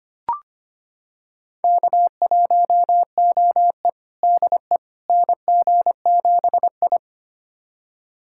Morse code, heard as a keyed tone, about 25 wpm, 700 Hz. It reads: K1OE DE NG7I
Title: USA Callsign DE Callsign - 4 Characters - 25wpm